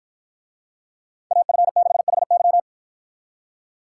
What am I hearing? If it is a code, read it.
Morse code, heard as a keyed tone, about 51 wpm, 700 Hz. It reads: AF6HY